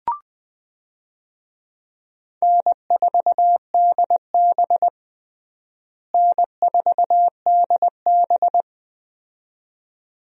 Morse code, heard as a keyed tone, about 20 wpm, 700 Hz. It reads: N4DB N4DB
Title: USA Call Signs - 4 Characters - Repeated 2x - 20wpm